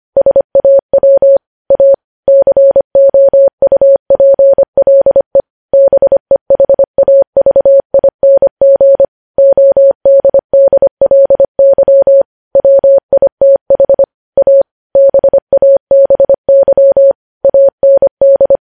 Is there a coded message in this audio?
Yes: Morse code from 0.2 to 18.6 s, heard as a keyed tone, about 25 wpm, 550 Hz. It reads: SAW A COUPLE BEHAVING ODDLY WITH A BABY AND